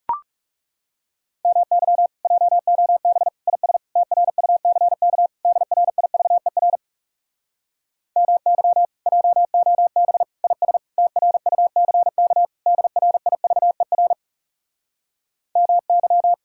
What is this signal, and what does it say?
Morse code, heard as a keyed tone, about 37 wpm, 700 Hz. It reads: MY JOB IS TRUCK DRIVER MY JOB IS TRUCK DRIVER MY
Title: Speed-Racing Common QSO Phrases - 30wpm